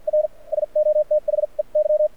Morse code, heard as a keyed tone, about 50 wpm, spaced as for 39 wpm, 600 Hz. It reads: R SO THEY
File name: archived / CRSnet